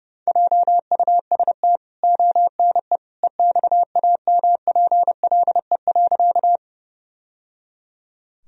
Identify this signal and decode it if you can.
Morse code, heard as a keyed tone, about 32 wpm, 700 Hz. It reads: JUST ONE EXAMPLE.
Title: Sentences from Top 1000 Words - Review Part 1 - 30wpm